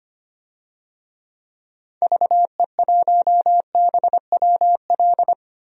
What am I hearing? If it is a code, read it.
Morse code, heard as a keyed tone, about 25 wpm, 700 Hz. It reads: VE1BWL
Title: DXCC-100 Call Signs - 6 Characters - 25wpm